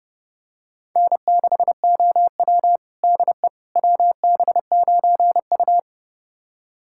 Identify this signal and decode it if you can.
Morse code, heard as a keyed tone, about 29 wpm, 700 Hz. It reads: N6OW DE WB9U
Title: USA Callsign DE Callsign - 4 Characters - 30wpm